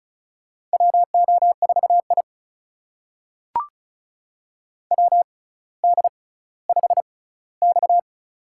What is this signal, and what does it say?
Morse code, heard as a keyed tone, about 36 wpm, spaced as for 18 wpm, 700 Hz. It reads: WO4I WD5X
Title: USA Call Signs - 4 Characters - 6x Character Spacing - 35wpm